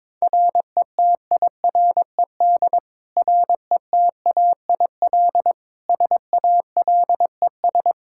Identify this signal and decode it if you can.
Morse code, heard as a keyed tone, about 22 wpm, 700 Hz. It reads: RETIRED RETAIL SALES
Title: Practice QSO - Exchange 3 - More Info - 22wpm